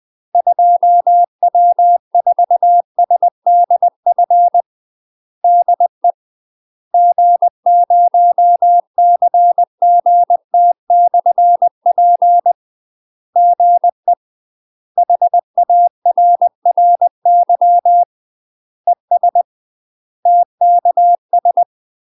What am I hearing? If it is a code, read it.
Morse code, heard as a keyed tone, about 20 wpm, 700 Hz. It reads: 2W4SDF DE G0CGT/P GE HARRY ES TKS
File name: UK MASTER QSOs TRAINING / 20WPM / UK MASTER QSOs 0046